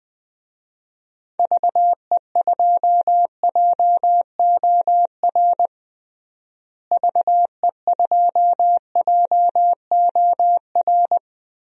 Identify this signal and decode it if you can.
Morse code, heard as a keyed tone, about 20 wpm, 700 Hz. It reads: VE2JOR VE2JOR
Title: DXCC-100 Call Signs - 6 Characters - Repeated 2x - 20wpm